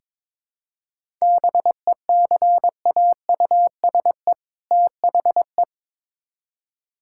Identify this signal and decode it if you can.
Morse code, heard as a keyed tone, about 22 wpm, 700 Hz. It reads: BECAUSE THE
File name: Build A Phrase - 2-Word Phrases - 22wpm